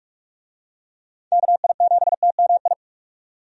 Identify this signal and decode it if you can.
Morse code, heard as a keyed tone, about 43 wpm, 700 Hz. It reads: KI7TMI
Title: USA Call Signs - 6 Characters - 45wpm